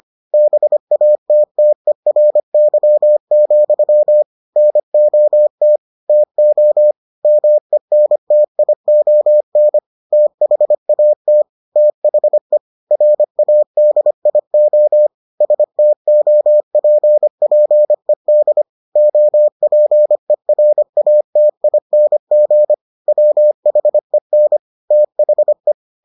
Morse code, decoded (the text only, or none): BATTERY, NOT TO MENTION THAT THE RADIO STOPPED OPERATING WHEN THE